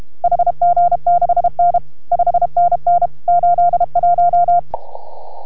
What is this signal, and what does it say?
Morse code, heard as a keyed tone, about 31 wpm, 690 Hz. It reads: HG6N 5NN 81